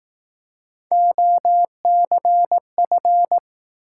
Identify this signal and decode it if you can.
Morse code, heard as a keyed tone, about 19 wpm, 700 Hz. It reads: OCF
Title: Speed-Racing Top 180 QSO Elements 15wpm